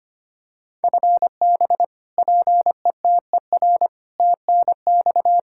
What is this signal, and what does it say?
Morse code, heard as a keyed tone, about 25 wpm, 700 Hz. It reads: FB PETER TNX
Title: Practice QSO - Exchange 3 - More Info - 25wpm